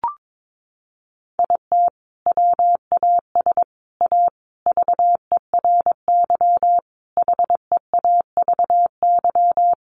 Morse code, heard as a keyed tone, about 22 wpm, 700 Hz. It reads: IT WAS A VERY HEAVY
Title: Sentences from Top 1100 Words 22wpm